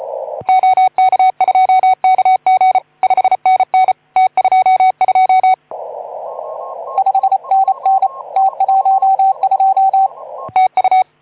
Morse code, heard as a keyed tone, about 33 wpm, 750 Hz. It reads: OK2KG 5NN T22 5NN T12 TU
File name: Archiv / CUC181 / OK2ZC CUC 181